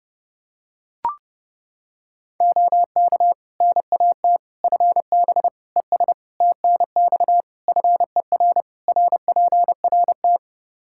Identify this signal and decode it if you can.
Morse code, heard as a keyed tone, about 30 wpm, 700 Hz. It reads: OK NAT FB ES TNX FER RPRT
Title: Practice QSO - Exchange 2 - Info - 30wpm